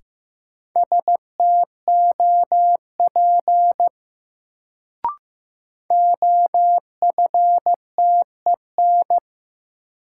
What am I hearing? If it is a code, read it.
Morse code, heard as a keyed tone, about 15 wpm, 700 Hz. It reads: STOP OFTEN
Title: Sentences from Top 300 Words - New Word Drill - 15wpm